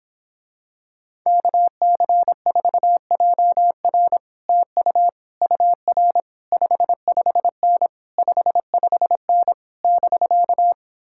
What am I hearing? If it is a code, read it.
Morse code, heard as a keyed tone, about 26 wpm, 700 Hz. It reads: KC4JR TU UR 55N 55N <BK>